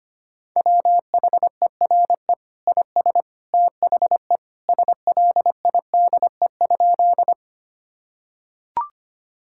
Morse code, heard as a keyed tone, about 25 wpm, 700 Hz. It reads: WHERE IS THE SLIDE?
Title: Sentences from Top 1600 Words 25wpm